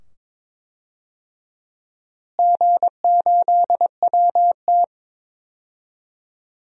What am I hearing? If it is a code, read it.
Morse code, heard as a keyed tone, about 22 wpm, 700 Hz. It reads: G8WT
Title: DXCC-100 Call Signs - 4 Characters - Repeated 2x - 22wpm